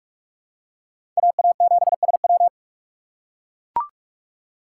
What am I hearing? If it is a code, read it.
Morse code, heard as a keyed tone, about 44 wpm, 690 Hz. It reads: AA7SW
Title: USA Call Signs - 4x Character Spacing - 45wpm